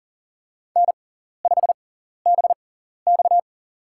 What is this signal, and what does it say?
Morse code, heard as a keyed tone, about 40 wpm, spaced as for 14 wpm, 700 Hz. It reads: N5BX
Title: USA Call Signs - 4 Characters - 6x Character Spacing - 40wpm